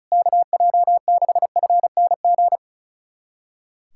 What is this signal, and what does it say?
Morse code, heard as a keyed tone, about 36 wpm, 700 Hz. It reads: KJ6FNG
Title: USA Callsign DE Callsign - 6 Characters - 35wpm